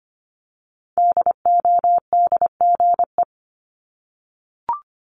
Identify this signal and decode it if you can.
Morse code, heard as a keyed tone, about 25 wpm, 700 Hz. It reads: DODGE
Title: Multi-letter Phonemes Repeated 3x 25wpm